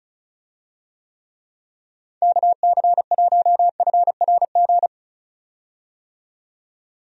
Morse code, heard as a keyed tone, about 34 wpm, 700 Hz. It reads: KC1FRG
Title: USA Call Signs - 6 Characters - 35wpm